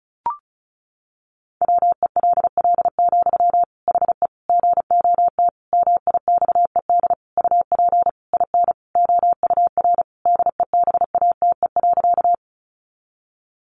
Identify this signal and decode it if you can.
Morse code, heard as a keyed tone, about 34 wpm, 700 Hz. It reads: WELL, HE GOT MIXED UP IN OUR DEBATE.
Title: Sentences from Top 1500 Words 35wpm